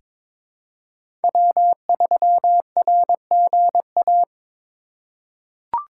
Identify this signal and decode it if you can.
Morse code, heard as a keyed tone, about 22 wpm, 700 Hz. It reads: W3RGA
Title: K1USN SST QSO Practice - 22wpm